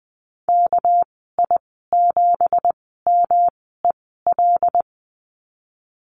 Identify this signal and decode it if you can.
Morse code, heard as a keyed tone, about 19 wpm, spaced as for 14 wpm, 700 Hz. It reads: KI7MEL
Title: 213 Lesson 34g - Number 8 - Call Signs 20wpm